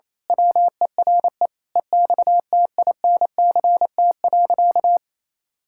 Morse code, heard as a keyed tone, about 28 wpm, 700 Hz. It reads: WERE EXTINCT.
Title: Pangram Challenge v2 - 28wpm